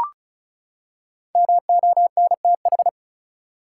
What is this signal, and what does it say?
Morse code, heard as a keyed tone, about 34 wpm, 700 Hz. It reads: MONTH
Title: Five Letter Words - 35wpm